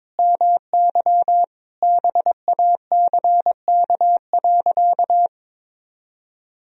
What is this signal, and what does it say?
Morse code, heard as a keyed tone, about 22 wpm, 700 Hz. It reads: MY BACK.
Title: Sentences from Top 1000 Words - Review Part 3 - 22wpm